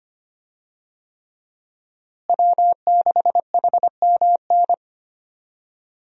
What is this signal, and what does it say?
Morse code, heard as a keyed tone, about 25 wpm, 700 Hz. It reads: W6HMN
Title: USA Call Signs - 5 Characters - Repeated 3x - 25wpm